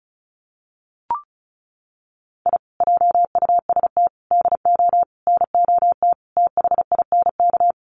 Morse code, heard as a keyed tone, about 34 wpm, 700 Hz. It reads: I JUST DO NOT THINK
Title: Sets of 5 Words Encore 35wpm